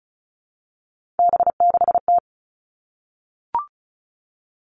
Morse code, heard as a keyed tone, about 36 wpm, 700 Hz. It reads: B6T
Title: Three Challenging Letter-Number Combinations 35wpm